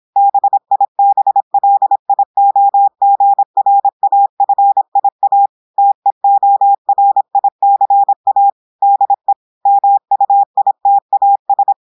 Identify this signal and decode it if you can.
Morse code, heard as a keyed tone, about 26 wpm, 800 Hz. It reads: BIBLIOGRAFIA TEORICA DE MUITAS
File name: audios / portugais / portugais texte en clair 1 vitesse 26 mots minutes